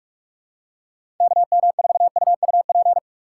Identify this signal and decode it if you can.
Morse code, heard as a keyed tone, about 44 wpm, 700 Hz. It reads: KM4UUP
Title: USA Call Signs - 6 Characters - 6x Character Spacing - 45wpm